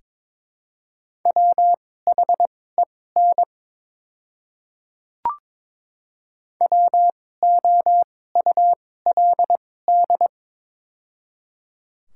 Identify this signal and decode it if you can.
Morse code, heard as a keyed tone, about 22 wpm, spaced as for 16 wpm, 700 Hz. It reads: WHEN WOULD
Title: 2x Character Spacing - Top 100 Words - 22wpm